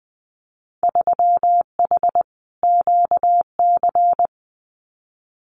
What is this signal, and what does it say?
Morse code, heard as a keyed tone, about 20 wpm, 700 Hz. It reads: 3H QC
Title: Winter Field Day Exchanges 20wpm